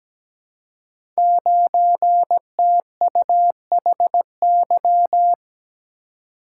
Morse code, heard as a keyed tone, about 17 wpm, 700 Hz. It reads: 9TUHY
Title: Code Groups - 5 Letter:Numbers - 17wpm